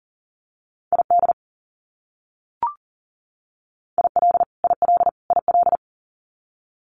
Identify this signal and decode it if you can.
Morse code, heard as a keyed tone, about 40 wpm, 700 Hz. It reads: ID IL IL IL
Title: US States Repeated 3x 40wpm